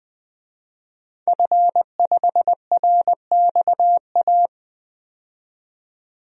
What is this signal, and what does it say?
Morse code, heard as a keyed tone, about 20 wpm, 700 Hz. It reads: F5RXA
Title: DXCC-100 Call Signs - 5 Characters - 20wpm